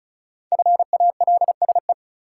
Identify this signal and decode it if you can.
Morse code, heard as a keyed tone, about 36 wpm, 700 Hz. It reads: FALSE
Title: Sentences from Top 1600 Words - New Word Drill Alt - 35wpm